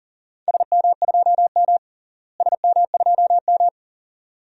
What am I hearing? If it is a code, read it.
Morse code, heard as a keyed tone, about 40 wpm, 700 Hz. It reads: SM2M SM2M
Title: DXCC-100 Call Signs - 4 Characters - Repeated 2x - 40wpm